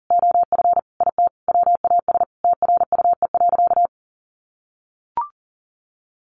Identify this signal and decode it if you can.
Morse code, heard as a keyed tone, about 40 wpm, 700 Hz. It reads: OF IT WAS TRUE.
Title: Sentences from Top 400 Words 40wpm